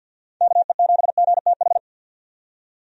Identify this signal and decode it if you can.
Morse code, heard as a keyed tone, about 51 wpm, 700 Hz. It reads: KE7GTH
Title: USA Call Signs - 6 Characters - 50wpm